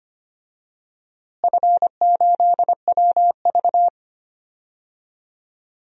Morse code, heard as a keyed tone, about 25 wpm, 700 Hz. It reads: F8WV
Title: DXCC-100 Call Signs - 4 Characters - Repeated 2x - 25wpm